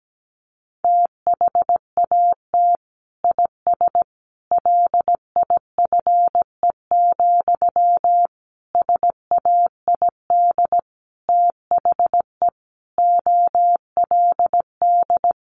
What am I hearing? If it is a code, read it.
Morse code, heard as a keyed tone, about 17 wpm, 700 Hz. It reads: THAT IS LIFE, SAID THE OLD
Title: Sentences from Top 900 Words 17wpm